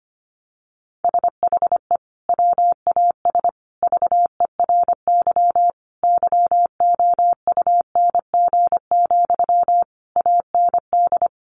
Morse code, heard as a keyed tone, about 25 wpm, 700 Hz. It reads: SHE WAS VERY YOUNG, AND
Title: Sentences from Top 1400 Words 25wpm